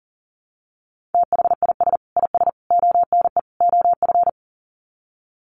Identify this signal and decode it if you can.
Morse code, heard as a keyed tone, about 40 wpm, 700 Hz. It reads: THIS IS ONE OF
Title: Sets of 4 Words 40wpm